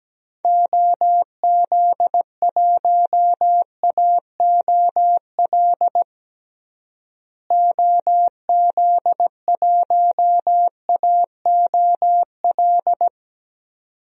Morse code, heard as a keyed tone, about 17 wpm, 700 Hz. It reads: OZ1AOL OZ1AOL
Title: DXCC-100 Call Signs - 6 Characters - Repeated 2x - 17wpm